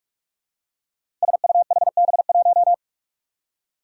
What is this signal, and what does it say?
Morse code, heard as a keyed tone, about 44 wpm, 700 Hz. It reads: SUHB1
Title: Five Challenging Letter-Number Combinations 45wpm